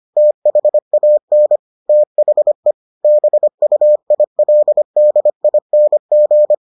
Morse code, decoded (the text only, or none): THAN THE BUILDING